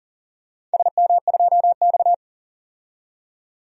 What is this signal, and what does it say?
Morse code, heard as a keyed tone, about 40 wpm, 700 Hz. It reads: SM2X
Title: DXCC-100 Call Signs - 4 Characters - 40wpm